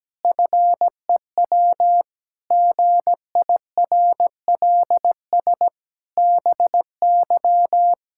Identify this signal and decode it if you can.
Morse code, heard as a keyed tone, about 17 wpm, 700 Hz. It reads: FEW GIRLS BY
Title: Pangram Challenge v2 - 17wpm